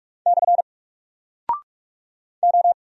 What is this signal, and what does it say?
Morse code, heard as a keyed tone, about 44 wpm, 700 Hz. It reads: / O
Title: Single Character (Letter:Number:Punctuation:Prosign) 45wpm